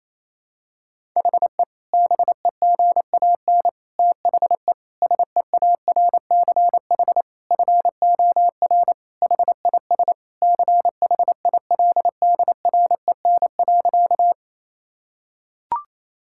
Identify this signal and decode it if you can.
Morse code, heard as a keyed tone, about 28 wpm, 700 Hz. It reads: HE BEGAN THE SEARCH FOR HIS CHILDREN.
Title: Sentences from Top 1000 Words - Review Part 5 - 28wpm